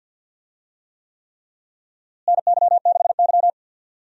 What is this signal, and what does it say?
Morse code, heard as a keyed tone, about 53 wpm, 700 Hz. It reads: NY6Y
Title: USA Call Signs - 4 Characters - Repeated 3x - 50wpm